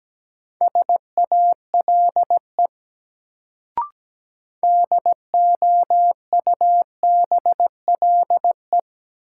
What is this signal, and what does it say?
Morse code, heard as a keyed tone, about 17 wpm, 700 Hz. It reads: SALE DOUBLE
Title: Sentences from Top 1400 Words - New Word Drill - 17wpm